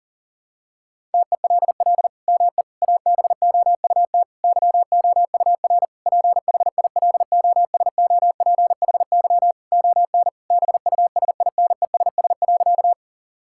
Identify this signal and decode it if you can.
Morse code, heard as a keyed tone, about 40 wpm, 700 Hz. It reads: TELL ME ABOUT YOUR PHILOSOPHY ON BUSINESS.
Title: Sentences from Top 1000 Words - Review Part 6 - 40wpm